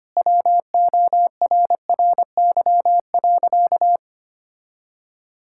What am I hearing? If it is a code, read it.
Morse code, heard as a keyed tone, about 25 wpm, 700 Hz. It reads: WORRY.
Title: Pangram Challenge v2 - 25wpm